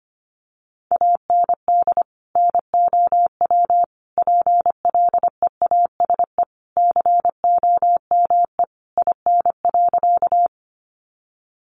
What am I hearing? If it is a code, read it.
Morse code, heard as a keyed tone, about 25 wpm, 700 Hz. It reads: AND NOW PLEASE COME IN.
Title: Sentences from Top 1500 Words 25wpm